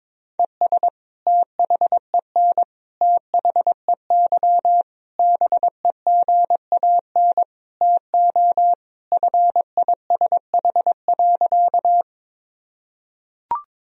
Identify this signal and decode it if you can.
Morse code, heard as a keyed tone, about 21 wpm, 700 Hz. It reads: ES THEN THEY BEGAN TO FISH.
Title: Sentences from Top 300 Words and Abbreviations 22wpm